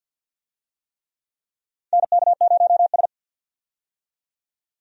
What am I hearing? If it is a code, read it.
Morse code, heard as a keyed tone, about 48 wpm, 700 Hz. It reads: NK0S
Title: USA Call Signs - 4 Characters - 50wpm